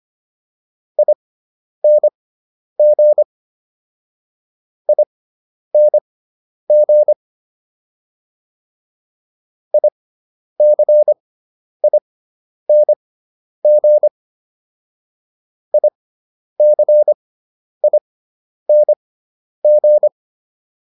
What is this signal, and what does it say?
Morse code, heard as a keyed tone, about 25 wpm, spaced as for 10 wpm, 600 Hz. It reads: ING ING ICING ICING